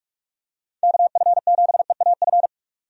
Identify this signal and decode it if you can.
Morse code, heard as a keyed tone, about 41 wpm, 700 Hz. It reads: KF7EAF
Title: USA Call Signs - 6 Characters - 45wpm